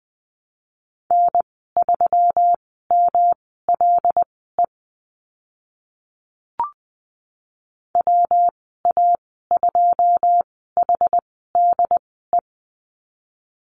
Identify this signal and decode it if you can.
Morse code, heard as a keyed tone, about 20 wpm, spaced as for 15 wpm, 700 Hz. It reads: N3MLE WA2HDE
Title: 161 Lesson 26g - Letter g - Call Signs 20wpm